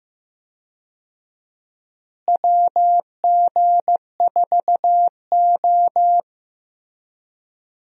Morse code, heard as a keyed tone, about 15 wpm, 700 Hz. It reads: WG4O
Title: USA Call Signs - 4 Characters - Repeated 2x - 15wpm